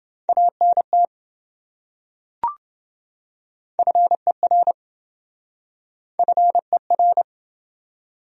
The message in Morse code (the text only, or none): ANT FER FER